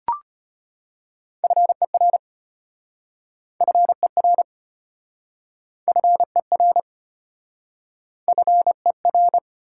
Morse code, heard as a keyed tone, about 33 wpm, 700 Hz. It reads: FER FER FER FER